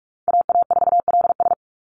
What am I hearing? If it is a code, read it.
Morse code, heard as a keyed tone, about 44 wpm, 700 Hz. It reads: AA4LS
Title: Call Signs Repeated 2x - 45wpm